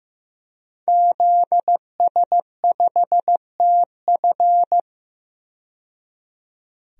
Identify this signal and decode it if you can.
Morse code, heard as a keyed tone, about 16 wpm, 700 Hz. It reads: ZS5TF
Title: DXCC-100 Call Signs - 5 Characters - 15wpm